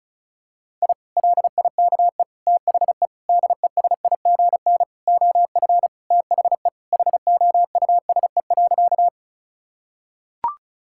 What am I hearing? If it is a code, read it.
Morse code, heard as a keyed tone, about 36 wpm, 700 Hz. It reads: I LIKE THE DESIGN OF THE HOUSE.
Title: Sentences from Top 1000 Words - Review Part 5 - 35wpm